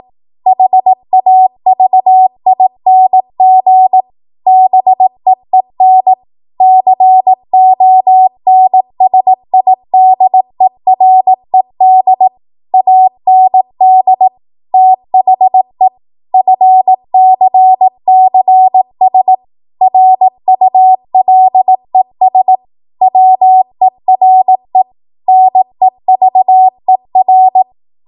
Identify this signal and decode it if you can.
Morse code, heard as a keyed tone, about 18 wpm, 750 Hz. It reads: HAVING BEEN CONSIDERED AND THE FCCS RULES WERE NEVER
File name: Morse / 18WPM